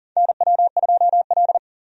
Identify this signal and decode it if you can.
Morse code, heard as a keyed tone, about 40 wpm, 700 Hz. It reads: NW2L
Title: USA Call Signs - 4 Characters - 40wpm